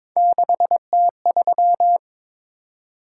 Morse code, heard as a keyed tone, about 22 wpm, 700 Hz. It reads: 6T3
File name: practice / 22wpm / Three Letter-Number Combinations - 22wpm